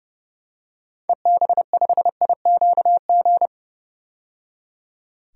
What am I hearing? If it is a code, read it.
Morse code, heard as a keyed tone, about 30 wpm, 700 Hz. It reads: EB5IQG
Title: DXCC-100 Call Signs - 6 Characters - 30wpm